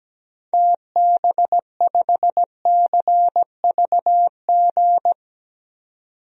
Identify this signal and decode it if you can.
Morse code, heard as a keyed tone, about 17 wpm, 700 Hz. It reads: TB5CVG
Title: DXCC-100 Call Signs - 6 Characters - 17wpm